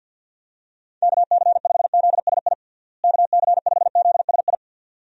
Morse code, heard as a keyed tone, about 51 wpm, 700 Hz. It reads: KC5ZSI KC5ZSI
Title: USA Call Signs - 6 Characters - Repeated 2x - 50wpm